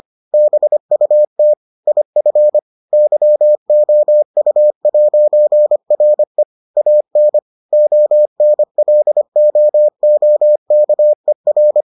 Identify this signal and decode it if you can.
Morse code, heard as a keyed tone, about 25 wpm, 600 Hz. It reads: BUT IF YOU'RE AN ONLOOKER